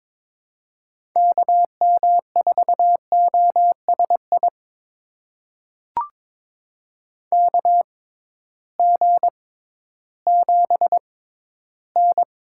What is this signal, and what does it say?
Morse code, heard as a keyed tone, about 22 wpm, spaced as for 13 wpm, 700 Hz. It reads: KM4OSI KG7N
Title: USA Call Signs - 6 Characters - 6x Character Spacing - 22wpm